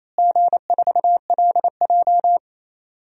Morse code, heard as a keyed tone, about 28 wpm, 700 Hz. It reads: G4LJ
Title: DXCC-100 Call Signs - 4 Characters - Repeated 3x - 28wpm